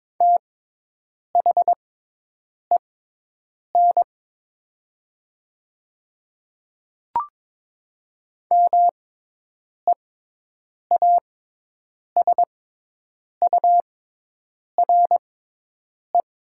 Morse code, heard as a keyed tone, about 22 wpm, spaced as for 8 wpm, 700 Hz. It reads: THEN MEASURE
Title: 6x Character Spacing - Top 500 Words - 22wpm